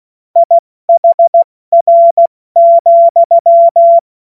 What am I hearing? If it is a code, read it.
Morse code, heard as a keyed tone, about 16 wpm, 680 Hz. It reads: IHR,